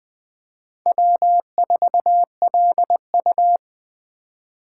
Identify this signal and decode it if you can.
Morse code, heard as a keyed tone, about 20 wpm, 700 Hz. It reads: W4LU